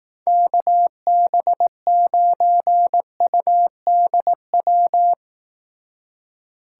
Morse code, heard as a keyed tone, about 18 wpm, 700 Hz. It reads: KB9UDW
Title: Speed Racing - USA 2x3 Call Signs 15wpm